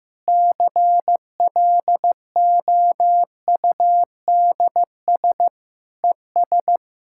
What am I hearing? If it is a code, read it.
Morse code, heard as a keyed tone, about 16 wpm, 700 Hz. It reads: CLOUDS ES
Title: Practice QSO - Exchange 2 - Info - 15wpm